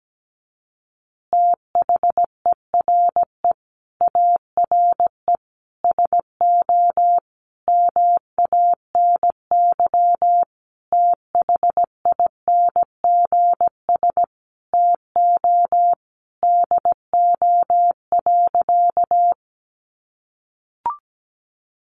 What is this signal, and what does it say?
Morse code, heard as a keyed tone, about 17 wpm, 700 Hz. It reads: THERE ARE SO MANY THINGS TO DO.
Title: Additional Easy Phrases 17wpm